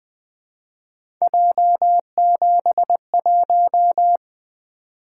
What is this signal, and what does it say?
Morse code, heard as a keyed tone, about 20 wpm, 700 Hz. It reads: J71
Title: Three Letter-Number Combinations - 20wpm